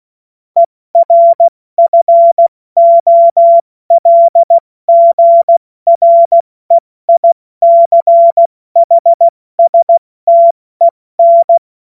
Morse code, heard as a keyed tone, about 16 wpm, 680 Hz. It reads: ERFOLGREICHSTEN
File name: Morsekurs / Witze / Witz 09. August 2022 70 F80